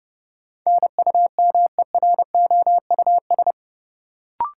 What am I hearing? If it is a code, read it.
Morse code, heard as a keyed tone, about 32 wpm, 700 Hz. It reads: NUMEROUS
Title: Sentences from Top 1400 Words - New Word Drill - 30wpm